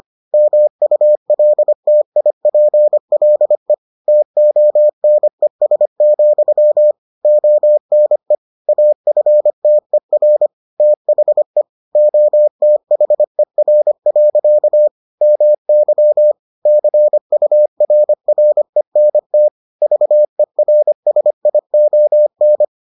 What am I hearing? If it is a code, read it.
Morse code, heard as a keyed tone, about 25 wpm, 600 Hz. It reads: MULTIPLE TONES, ONE AFTER THE OTHER. MY CURRENT VERSION